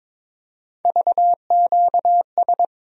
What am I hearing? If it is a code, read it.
Morse code, heard as a keyed tone, about 22 wpm, 700 Hz. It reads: VQS
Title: USA Call Sign Postfixes - 3 Letters 22wpm